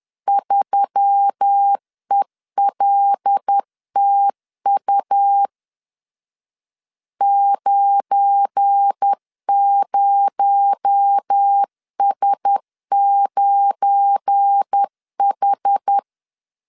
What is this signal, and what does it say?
Morse code, heard as a keyed tone, about 10 wpm, 800 Hz. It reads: 3ELTU 90S9H
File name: LEZIONI / 7 GF9Z / 7 GF9 10wpm 2747